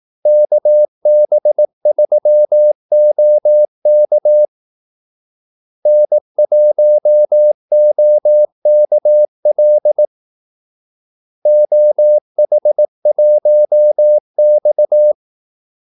Morse code, heard as a keyed tone, about 18 wpm, 600 Hz. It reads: KB3OK N1OKL OH1X
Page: Kaldesignaler 18 wpm